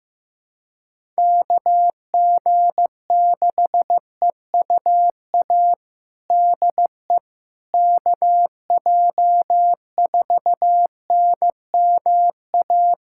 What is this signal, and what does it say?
Morse code, heard as a keyed tone, about 15 wpm, 700 Hz. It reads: KG6EUA DE KJ4NMA
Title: USA Callsign DE Callsign - 6 Characters - 15wpm